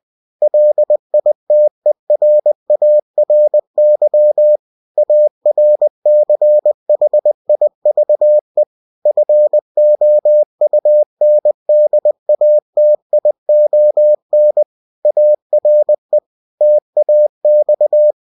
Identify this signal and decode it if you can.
Morse code, heard as a keyed tone, about 20 wpm, 600 Hz. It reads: LITERARY ARCHIVE FOUNDATION ARE TAX
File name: cw-books / winnie-the-pooh / winnie-20wpm-page-0121